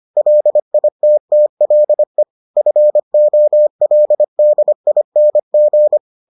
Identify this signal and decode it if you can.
Morse code, heard as a keyed tone, about 25 wpm, 600 Hz. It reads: LITTLE FOLDING